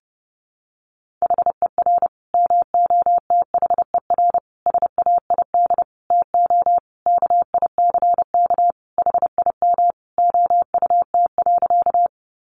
Morse code, heard as a keyed tone, about 28 wpm, 700 Hz. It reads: HER MOTHER SAID TO KICK HIM OUT.
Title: Sentences from Top 1300 Words 30wpm